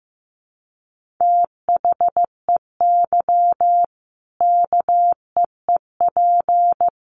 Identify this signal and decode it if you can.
Morse code, heard as a keyed tone, about 15 wpm, 700 Hz. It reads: THEY KEEP
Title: Sentences from Top 1500 Words 15wpm